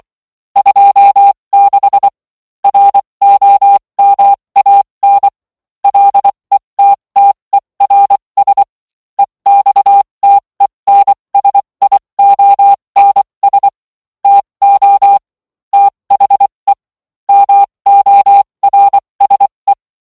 Morse code, heard as a keyed tone, about 23 wpm, spaced as for 20 wpm, 800 Hz. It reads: 26 ROMAN LETTERS EXTENSIONS TO THE MORSE